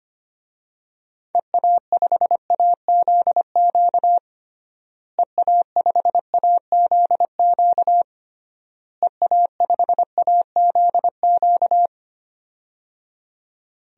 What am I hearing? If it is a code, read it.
Morse code, heard as a keyed tone, about 25 wpm, 700 Hz. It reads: EA5AZQ EA5AZQ EA5AZQ
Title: DXCC-100 Call Signs - 6 Characters - Repeated 3x - 25wpm